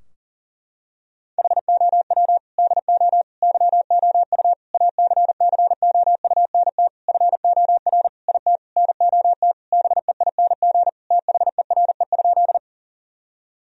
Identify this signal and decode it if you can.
Morse code, heard as a keyed tone, about 40 wpm, 700 Hz. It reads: HOW DO YOU ACCOUNT FOR IT NOT BEING THERE?
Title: Sentences from Top 1000 Words - Review Part 3 - 40wpm